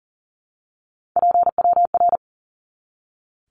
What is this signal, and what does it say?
Morse code, heard as a keyed tone, about 40 wpm, 700 Hz. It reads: PWR
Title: Abbreviations 40wpm